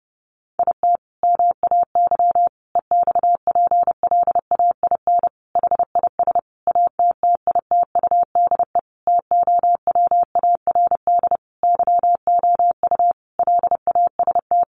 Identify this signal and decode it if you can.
Morse code, heard as a keyed tone, about 30 wpm, 700 Hz. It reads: IT MAY EXPLAIN HIS ATTITUDE TOWARD YOU LAST
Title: Sentences from Top 1300 Words 30wpm